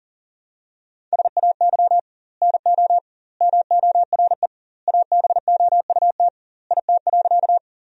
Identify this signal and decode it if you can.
Morse code, heard as a keyed tone, about 40 wpm, 700 Hz. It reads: SAY NO MORE ABOUT IT.
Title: 2x Word-Spacing - Sentences from Top 200 Words - 40wpm